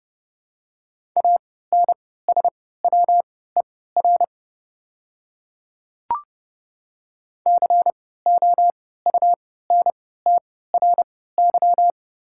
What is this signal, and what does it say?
Morse code, heard as a keyed tone, about 30 wpm, spaced as for 17 wpm, 700 Hz. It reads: ANSWER COUNTRY
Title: 3x Character Spacing - Top 300 Words - 30wpm